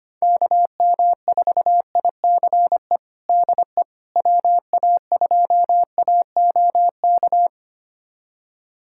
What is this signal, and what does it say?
Morse code, heard as a keyed tone, about 25 wpm, 700 Hz. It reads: KM4ICE DE WA2AOK
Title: USA Callsign DE Callsign - 6 Characters - 25wpm